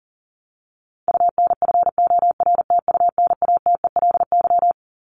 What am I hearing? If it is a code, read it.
Morse code, heard as a keyed tone, about 40 wpm, 700 Hz. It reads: UNFORTUNATELY
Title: Top 1751-2000 Words 40wpm